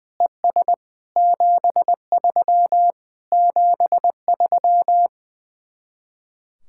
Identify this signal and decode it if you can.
Morse code, heard as a keyed tone, about 19 wpm, 700 Hz. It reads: ES 73 73
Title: Speed-Racing Common QSO Phrases - 20wpm